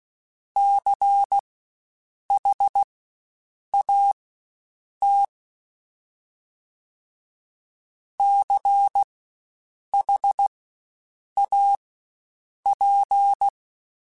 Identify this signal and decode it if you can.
Morse code, heard as a keyed tone, about 16 wpm, spaced as for 7 wpm, 800 Hz. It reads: CHAT CHAP
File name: Spaced Alphas / Spaced Alphas mp3s / Cs sp @15